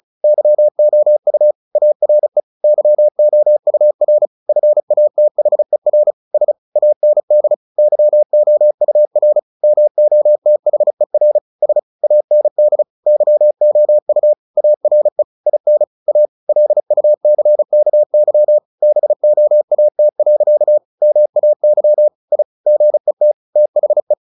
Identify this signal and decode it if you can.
Morse code, heard as a keyed tone, about 34 wpm, 600 Hz. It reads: YOU ARE YOUR FATHER S AND YOUR MOTHER S AND YOU ARE IN A LUCKY BOAT. MAY I GET THE
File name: the old man and the sea / the old man and the sea 1 35wpm